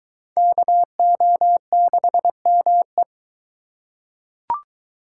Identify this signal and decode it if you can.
Morse code, heard as a keyed tone, about 21 wpm, 700 Hz. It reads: KO6ME